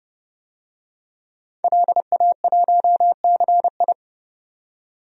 Morse code, heard as a keyed tone, about 30 wpm, 700 Hz. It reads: LA1CI
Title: DXCC-100 Call Signs - 5 Characters - 30wpm